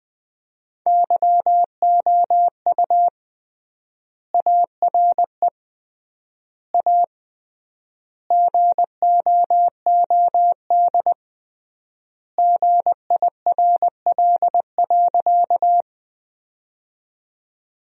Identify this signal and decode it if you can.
Morse code, heard as a keyed tone, about 20 wpm, 700 Hz. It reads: YOU ARE A GOOD GIRL.
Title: 3x Word-Spacing - Sentences from Top 400 Words - 20wpm